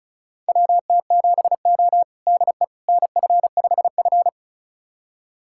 Morse code, heard as a keyed tone, about 36 wpm, 700 Hz. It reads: WT7O DE NF5F
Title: USA Callsign DE Callsign - 4 Characters - 35wpm